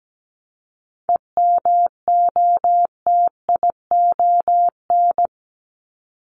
Morse code, heard as a keyed tone, about 17 wpm, 700 Hz. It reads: EMOTION
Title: Top 1501-1750 Words 17wpm